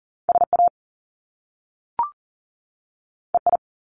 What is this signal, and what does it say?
Morse code, heard as a keyed tone, about 40 wpm, 700 Hz. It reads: SA EI
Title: Top Two Letter Combinations in Words 40wpm